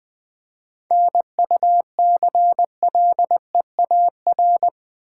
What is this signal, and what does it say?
Morse code, heard as a keyed tone, about 20 wpm, 700 Hz. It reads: NUCLEAR
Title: 7 Letter Words - First 3 Letters Given - 20wpm